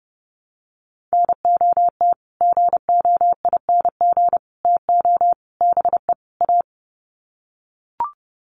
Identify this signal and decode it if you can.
Morse code, heard as a keyed tone, about 30 wpm, 700 Hz. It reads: NOT GOING TO BE A
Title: Sets of 5 Words Encore 30wpm